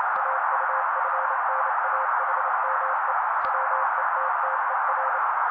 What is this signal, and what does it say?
Morse code, heard as a keyed tone, about 27 wpm, 530 Hz. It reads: AUFNAHME WETTER
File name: daily / 27wpm noise